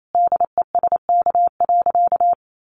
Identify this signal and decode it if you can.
Morse code, heard as a keyed tone, about 28 wpm, 700 Hz. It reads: DESK.
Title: A Sound Of Thunder 28wpm